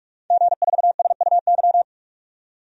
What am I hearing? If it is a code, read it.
Morse code, heard as a keyed tone, about 46 wpm, 700 Hz. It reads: G4SUY
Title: DXCC-100 Call Signs - 5 Characters - 45wpm